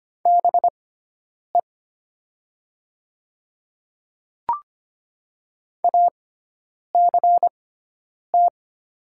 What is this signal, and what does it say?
Morse code, heard as a keyed tone, about 25 wpm, spaced as for 9 wpm, 690 Hz. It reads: BE ACT
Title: 6x Character Spacing - Top 200 Words - 25wpm